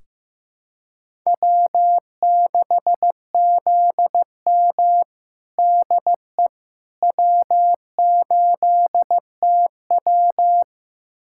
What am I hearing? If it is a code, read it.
Morse code, heard as a keyed tone, about 15 wpm, 700 Hz. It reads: W6ZM DE W8TW
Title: USA Callsign DE Callsign - 4 Characters - 15wpm